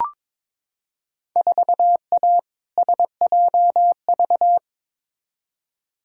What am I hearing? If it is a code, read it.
Morse code, heard as a keyed tone, about 22 wpm, 700 Hz. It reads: 4A SJV
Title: Field Day Exchanges Encore - 22wpm